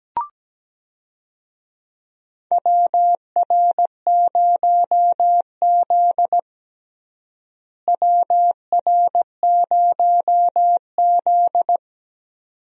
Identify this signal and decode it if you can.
Morse code, heard as a keyed tone, about 17 wpm, 700 Hz. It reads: WR0Z WR0Z
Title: USA Call Signs - 4 Characters - Repeated 2x - 17wpm